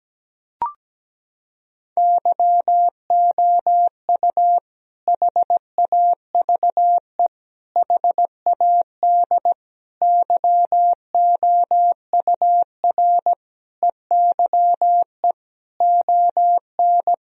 Morse code, heard as a keyed tone, about 17 wpm, 700 Hz. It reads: YOU HAVE HAD YOUR EYE ON
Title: Sentences from Top 1000 Words - Review Part 1 - 17wpm